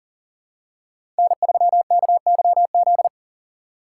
Morse code, heard as a keyed tone, about 40 wpm, 700 Hz. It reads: N3KYZ
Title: USA Call Signs - 5 Characters - 3x Character Spacing - 40wpm